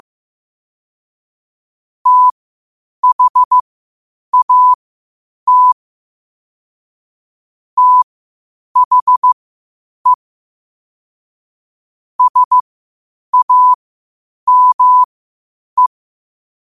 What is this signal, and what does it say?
Morse code, heard as a keyed tone, about 15 wpm, spaced as for 8 wpm, 1000 Hz. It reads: THAT THE SAME
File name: qotdfiles / 8wpm